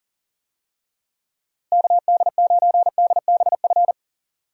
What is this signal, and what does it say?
Morse code, heard as a keyed tone, about 40 wpm, 700 Hz. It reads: KD9DBF
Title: USA Call Signs - 6 Characters - Repeated 2x - 40wpm